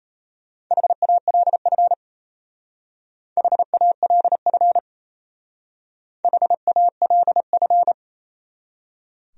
Morse code, heard as a keyed tone, about 34 wpm, 700 Hz. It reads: HALF HALF HALF
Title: Speed-Racing Top 500 Words 28wpm